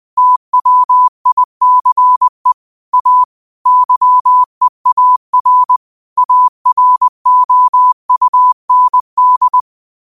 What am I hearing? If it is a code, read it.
Morse code, heard as a keyed tone, about 20 wpm, 1000 Hz. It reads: TWICE A YEAR AROUND
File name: qotdfiles / 20wpm